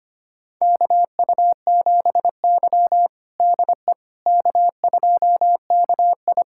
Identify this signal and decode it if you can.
Morse code, heard as a keyed tone, about 25 wpm, 700 Hz. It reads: KU7Y DE K2KI
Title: USA Callsign DE Callsign - 4 Characters - 25wpm